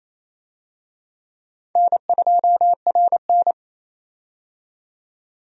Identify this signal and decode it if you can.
Morse code, heard as a keyed tone, about 28 wpm, 700 Hz. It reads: N2RN
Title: USA Call Signs - 4 Characters - 28wpm